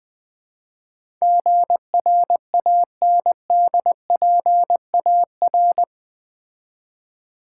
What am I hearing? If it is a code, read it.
Morse code, heard as a keyed tone, about 20 wpm, 700 Hz. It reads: GRANDPAR
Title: Build A Word - 4 Letters At A Time - 11-Letter Words - 20wpm